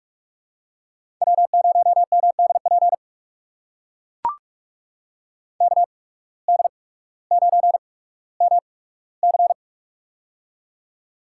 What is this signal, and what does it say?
Morse code, heard as a keyed tone, about 44 wpm, spaced as for 19 wpm, 700 Hz. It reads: W0MDP KD9MC
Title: USA Call Signs - 5 Characters - 8x Character Spacing - 45wpm